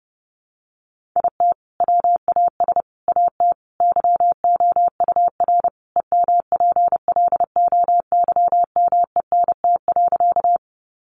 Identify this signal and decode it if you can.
Morse code, heard as a keyed tone, about 30 wpm, 700 Hz. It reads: IT WAS AT YOUR EMPLOYMENT.
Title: Sentences from Top 1500 Words 30wpm